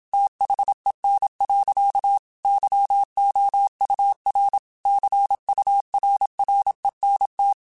Morse code, heard as a keyed tone, about 27 wpm, 800 Hz. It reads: THEN. YOUR CURRENT